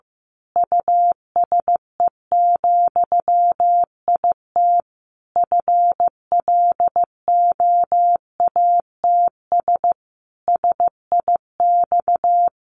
Morse code, heard as a keyed tone, about 15 wpm, 700 Hz. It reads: USE, IT FLOATS SIX
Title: A Sound Of Thunder 15wpm